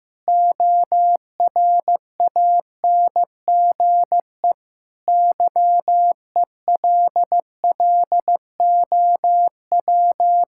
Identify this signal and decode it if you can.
Morse code, heard as a keyed tone, about 15 wpm, 700 Hz. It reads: ORANGE YELLOW
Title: Pangram Challenge v2 - 15wpm